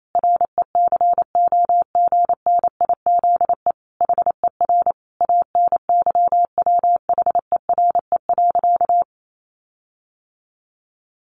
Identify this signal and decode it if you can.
Morse code, heard as a keyed tone, about 28 wpm, 700 Hz. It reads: RECOGNIZE HER ANYWHERE.
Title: Sentences from Top 1400 Words 28wpm